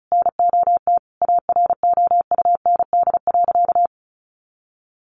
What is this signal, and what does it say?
Morse code, heard as a keyed tone, about 34 wpm, 700 Hz. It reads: NOT AROUND.
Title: Sentences from Top 1300 Words 35wpm